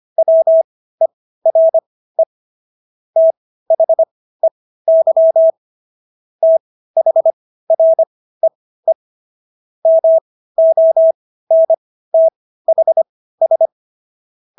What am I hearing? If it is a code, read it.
Morse code, heard as a keyed tone, about 25 wpm, spaced as for 15 wpm, 650 Hz. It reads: WERE THEY THREE MONTHS